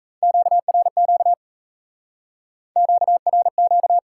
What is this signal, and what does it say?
Morse code, heard as a keyed tone, about 36 wpm, 700 Hz. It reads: QRQ QRQ